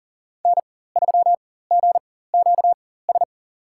Morse code, heard as a keyed tone, about 40 wpm, spaced as for 19 wpm, 700 Hz. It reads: N3GQS
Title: USA Call Signs - 5 Characters - 4x Character Spacing - 40wpm